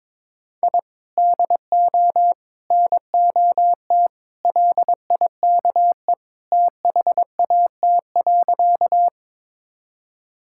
Morse code, heard as a keyed tone, about 22 wpm, 700 Hz. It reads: I DO NOT LIKE THAT.
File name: practice / 22wpm / Sentences with 4-Word Phrases - 22wpm